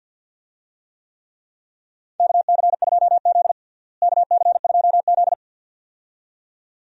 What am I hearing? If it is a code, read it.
Morse code, heard as a keyed tone, about 47 wpm, 690 Hz. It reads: KC2Z KC2Z
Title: USA Call Signs - 4 Characters - Repeated 2x - 50wpm